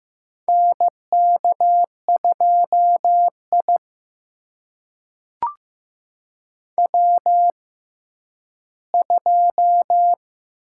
Morse code, heard as a keyed tone, about 15 wpm, spaced as for 10 wpm, 700 Hz. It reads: NK2I W2
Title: USA Call Signs - 4 Characters - 6x Character Spacing - 15wpm